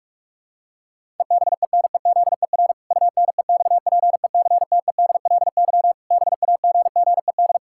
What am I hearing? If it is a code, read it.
Morse code, heard as a keyed tone, about 41 wpm, 700 Hz. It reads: EBENEZER UNEXPECTEDLY BAGGED
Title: Pangram Challenge v2 - 45wpm